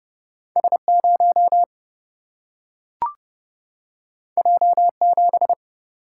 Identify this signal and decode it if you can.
Morse code, heard as a keyed tone, about 30 wpm, 700 Hz. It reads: S0 J7
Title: Single-Letter Single-Number 30wpm